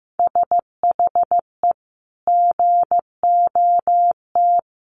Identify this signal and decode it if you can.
Morse code, heard as a keyed tone, about 15 wpm, 700 Hz. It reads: SHE GOT
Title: Sentences from Top 500 Words 15wpm